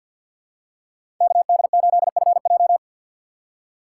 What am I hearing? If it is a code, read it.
Morse code, heard as a keyed tone, about 53 wpm, 700 Hz. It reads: KD8FJ
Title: USA Call Signs - 5 Characters - 3x Character Spacing - 50wpm